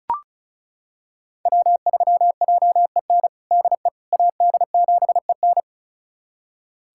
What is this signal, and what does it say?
Morse code, heard as a keyed tone, about 36 wpm, 700 Hz. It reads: W3JEN DE AD7EN
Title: USA Callsign DE Callsign - 5 Characters - 35wpm